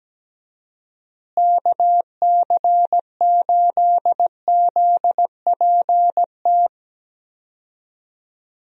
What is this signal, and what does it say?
Morse code, heard as a keyed tone, about 17 wpm, 700 Hz. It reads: KC8ZPT
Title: USA Call Signs - 6 Characters - Repeated 3x - 17wpm